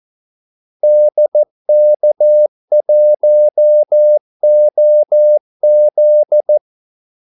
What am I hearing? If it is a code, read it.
Morse code, heard as a keyed tone, about 14 wpm, 600 Hz. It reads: DK1OZ
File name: Callsigns-14wpm